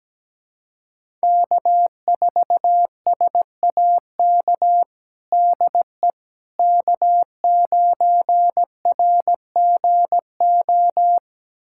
Morse code, heard as a keyed tone, about 17 wpm, 700 Hz. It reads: K4SAK DE K9RGO
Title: USA Callsign DE Callsign - 5 Characters - 17wpm